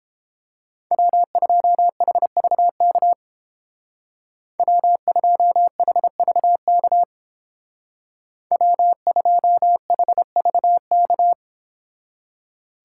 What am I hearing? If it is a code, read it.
Morse code, heard as a keyed tone, about 30 wpm, 700 Hz. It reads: W2HVK W2HVK W2HVK